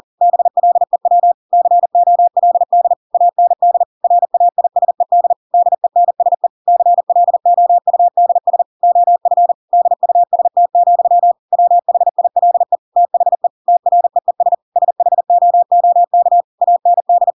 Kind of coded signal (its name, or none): Morse code